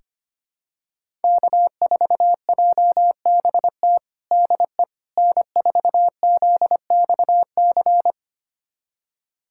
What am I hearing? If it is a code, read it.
Morse code, heard as a keyed tone, about 25 wpm, 700 Hz. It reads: K4JBT DE N4ZXC
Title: USA Callsign DE Callsign - 5 Characters - 25wpm